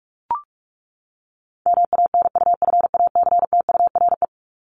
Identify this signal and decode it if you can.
Morse code, heard as a keyed tone, about 44 wpm, 700 Hz. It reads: MANUFACTURE
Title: ING Suffix - Top 100 Words with Base Words 45wpm